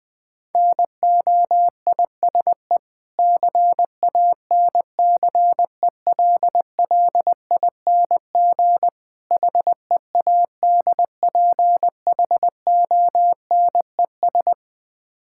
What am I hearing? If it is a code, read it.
Morse code, heard as a keyed tone, about 20 wpm, 700 Hz. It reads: NOISE CANCELLING HEADPHONES
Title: Difficult Words with Hints - 20wpm